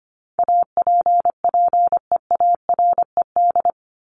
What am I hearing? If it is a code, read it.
Morse code, heard as a keyed tone, about 25 wpm, 700 Hz. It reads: APPEARED
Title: ED Suffix - Top 100 Words with Base Words 25wpm